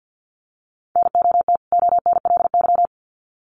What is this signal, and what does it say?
Morse code, heard as a keyed tone, about 50 wpm, 700 Hz. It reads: NOT ONLY
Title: Sets of 2 Words 50wpm